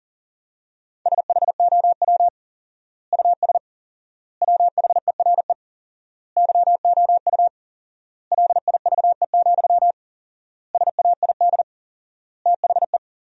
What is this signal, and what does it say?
Morse code, heard as a keyed tone, about 40 wpm, 700 Hz. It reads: SHOW US WHERE YOU LIVE, SAID THE
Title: 4x Word-Spacing - Sentences from Top 200 Words - 40wpm